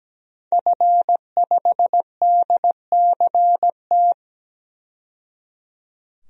Morse code, heard as a keyed tone, about 17 wpm, 700 Hz. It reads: F5DCT
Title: DXCC-100 Call Signs - 5 Characters - 17wpm